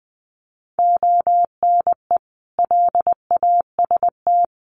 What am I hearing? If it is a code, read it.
Morse code, heard as a keyed tone, about 19 wpm, 700 Hz. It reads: ONE LAST
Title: Sentences from Top 1200 Words 20wpm